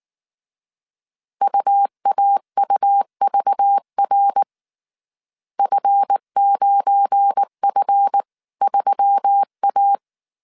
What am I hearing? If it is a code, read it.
Morse code, heard as a keyed tone, about 19 wpm, 800 Hz. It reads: UAUVR F9F3A
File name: CW / DET / 1 9 1500C S A 18wpm 2459